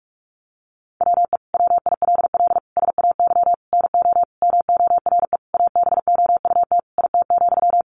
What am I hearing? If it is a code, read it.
Morse code, heard as a keyed tone, about 44 wpm, 700 Hz. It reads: WE WILL SAY NO MORE ABOUT IT,
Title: Sentences from Top 200 Words 45wpm